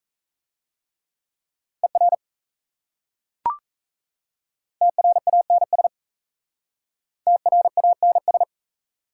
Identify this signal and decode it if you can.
Morse code, heard as a keyed tone, about 37 wpm, 700 Hz. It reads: ER TRANS TRANS